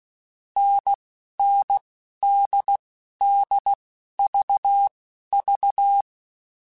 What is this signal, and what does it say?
Morse code, heard as a keyed tone, about 16 wpm, spaced as for 11 wpm, 800 Hz. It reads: NNDDVV
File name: Short Runs / Short Runs-mp3s / NDB NDB6 @10